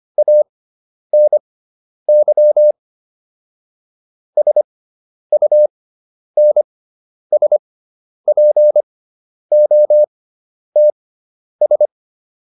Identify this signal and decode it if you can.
Morse code, heard as a keyed tone, about 25 wpm, spaced as for 10 wpm, 600 Hz. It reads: ANY SUNSPOTS